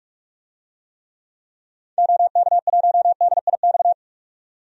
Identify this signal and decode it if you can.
Morse code, heard as a keyed tone, about 46 wpm, 690 Hz. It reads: OK1DIX
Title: DXCC-100 Call Signs - 6 Characters - 45wpm